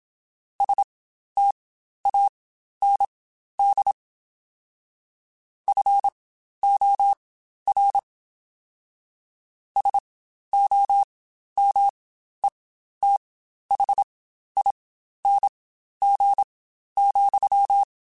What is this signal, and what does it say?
Morse code, heard as a keyed tone, about 27 wpm, spaced as for 12 wpm, 800 Hz. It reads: STAND FOR SOMETHING,
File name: Quotes Spaced / Quotes Spaced-mp3s / StandFor sp @25